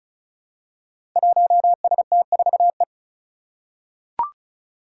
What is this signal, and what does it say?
Morse code, heard as a keyed tone, about 36 wpm, 700 Hz. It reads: 1ST4E
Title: Five Challenging Letter-Number Combinations 35wpm